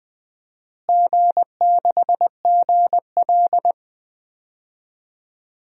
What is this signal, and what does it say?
Morse code, heard as a keyed tone, about 21 wpm, 700 Hz. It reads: G6GL
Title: DXCC-100 Call Signs - 4 Characters - Repeated 2x - 20wpm